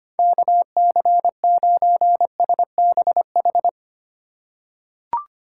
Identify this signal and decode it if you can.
Morse code, heard as a keyed tone, about 25 wpm, 700 Hz. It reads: KC9SBH
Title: USA Call Signs - 6 Characters - 4x Character Spacing - 25wpm